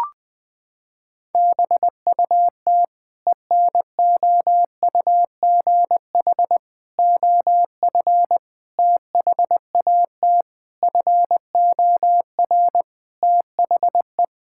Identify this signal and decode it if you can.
Morse code, heard as a keyed tone, about 21 wpm, 700 Hz. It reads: BUT ENOUGH OF THAT FOR THE
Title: Sentences from Top 1000 Words - Review Part 5 - 20wpm